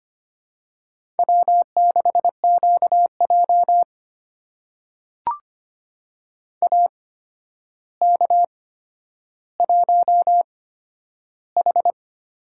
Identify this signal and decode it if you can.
Morse code, heard as a keyed tone, about 25 wpm, spaced as for 11 wpm, 700 Hz. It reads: W6QJ AK1H
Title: USA Call Signs - 8x Character Spacing - 25wpm